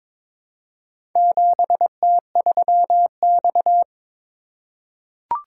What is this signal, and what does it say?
Morse code, heard as a keyed tone, about 22 wpm, 700 Hz. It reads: 7T3X
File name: practice / 22wpm / Four Letter-Number Combinations - 22wpm